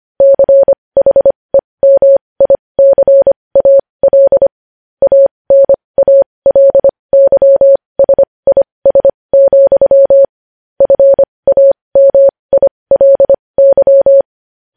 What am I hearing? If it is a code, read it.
Morse code, heard as a keyed tone, about 25 wpm, spaced as for 20 wpm, 550 Hz. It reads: CHEMICAL ANALYSIS, FAMILY